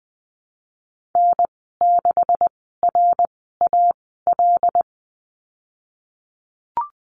Morse code, heard as a keyed tone, about 20 wpm, spaced as for 14 wpm, 700 Hz. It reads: N6RAL
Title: 186 Lesson 30g - Letter b - Call Signs 20wpm